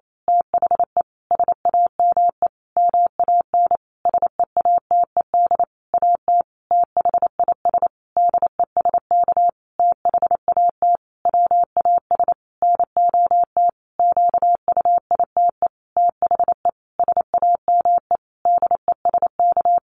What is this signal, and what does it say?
Morse code, heard as a keyed tone, about 28 wpm, 700 Hz. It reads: THE SAME MAN SEATED AT THIS DESK THAT WAS NOT QUITE THE SAME DESK
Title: A Sound Of Thunder 28wpm